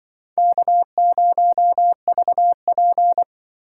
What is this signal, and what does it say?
Morse code, heard as a keyed tone, about 24 wpm, 700 Hz. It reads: K0VP
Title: Speed Racing - USA 1x2 Call Signs 20wpm